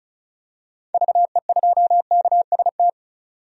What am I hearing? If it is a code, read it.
Morse code, heard as a keyed tone, about 34 wpm, 700 Hz. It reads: VE2KST
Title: DXCC-100 Call Signs - 6 Characters - 35wpm